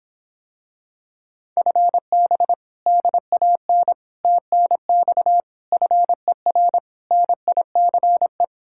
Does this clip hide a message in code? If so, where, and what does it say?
Yes: Morse code from 1.6 to 8.4 s, heard as a keyed tone, about 26 wpm, 700 Hz. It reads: FB DAN TNX FER NICE